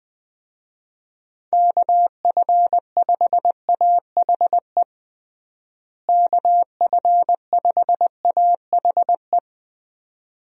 Morse code, heard as a keyed tone, about 20 wpm, 700 Hz. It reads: KF5AHE KF5AHE
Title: USA Call Signs - 6 Characters - Repeated 2x - 20wpm